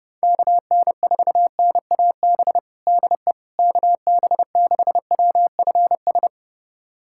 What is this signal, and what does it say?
Morse code, heard as a keyed tone, about 29 wpm, 700 Hz. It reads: KN4NAB DE KB6WFS
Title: USA Callsign DE Callsign - 6 Characters - 30wpm